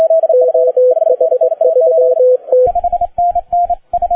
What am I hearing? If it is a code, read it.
Morse code, heard as a keyed tone, about 23 wpm, 500 Hz, with radio noise. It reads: OH3T
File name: wrtc02 20cw hb9czf